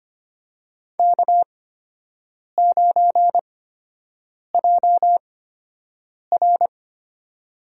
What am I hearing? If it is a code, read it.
Morse code, heard as a keyed tone, about 25 wpm, spaced as for 7 wpm, 700 Hz. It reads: K9JR
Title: USA Call Signs - 8x Character Spacing - 25wpm